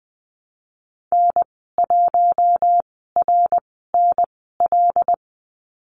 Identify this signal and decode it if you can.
Morse code, heard as a keyed tone, about 20 wpm, spaced as for 15 wpm, 700 Hz. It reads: N1RNL
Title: 179 Lesson 29g - Slash - Call Signs 20wpm